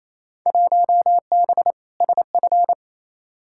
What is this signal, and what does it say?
Morse code, heard as a keyed tone, about 28 wpm, 700 Hz. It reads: 1B SF
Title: Field Day Exchanges Encore - 28wpm